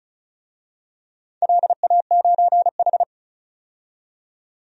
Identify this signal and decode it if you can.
Morse code, heard as a keyed tone, about 36 wpm, 700 Hz. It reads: LA9H
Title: DXCC-100 Call Signs - 4 Characters - 35wpm